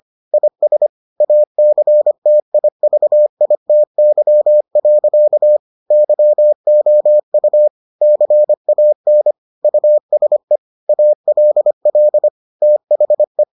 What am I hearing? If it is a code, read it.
Morse code, heard as a keyed tone, about 25 wpm, 600 Hz. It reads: IS ACTIVITY. YOU CAN USE ALL THE